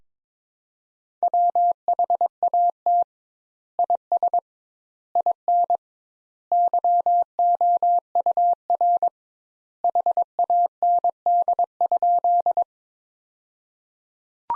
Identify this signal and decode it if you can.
Morse code, heard as a keyed tone, about 21 wpm, 700 Hz. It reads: WHAT IS IN YOUR HAND?
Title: Sentences Top 200 Words - 22wpm